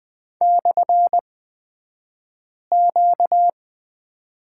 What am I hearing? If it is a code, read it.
Morse code, heard as a keyed tone, about 20 wpm, spaced as for 5 wpm, 700 Hz. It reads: /Q
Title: 235 Lesson 38d - Letter q - Two Characters 20wpm